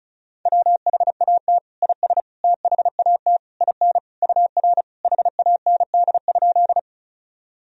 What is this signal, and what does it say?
Morse code, heard as a keyed tone, about 36 wpm, 700 Hz. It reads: WHAT IS THAT IN UR HAND?
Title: Sentences from Top 200 Words and Abbreviations 35wpm